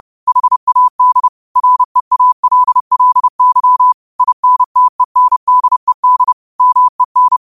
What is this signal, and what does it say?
Morse code, heard as a keyed tone, about 30 wpm, 1000 Hz. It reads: HAD REALLY INTENDED MEN